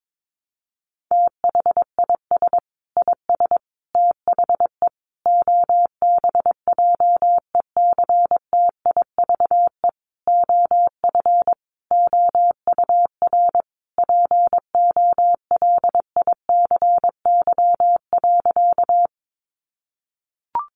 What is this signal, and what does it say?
Morse code, heard as a keyed tone, about 22 wpm, 700 Hz. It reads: THIS IS THE OBJECTIVE OF OUR POLICY.
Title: Sentences from Top 1200 Words 22wpm